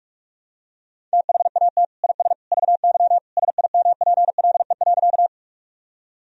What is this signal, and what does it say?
Morse code, heard as a keyed tone, about 46 wpm, 700 Hz. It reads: THAT IS VY SIMPLE.
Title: Sentences from Top 400 Words and Abbreviations 45wpm